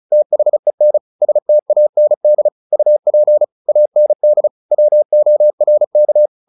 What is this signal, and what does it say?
Morse code, heard as a keyed tone, about 34 wpm, 600 Hz. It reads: THEN STAND UP AND WORK
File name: the old man and the sea 2 35wpm